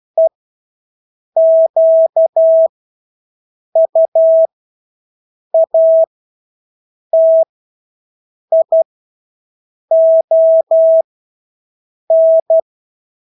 Morse code, heard as a keyed tone, about 12 wpm, spaced as for 6 wpm, 650 Hz. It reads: EQUATION
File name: Engelske ord 12-6 wpm